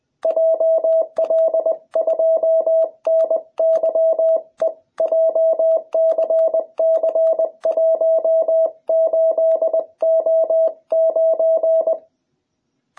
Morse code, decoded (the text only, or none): JL2NYE JCC18O9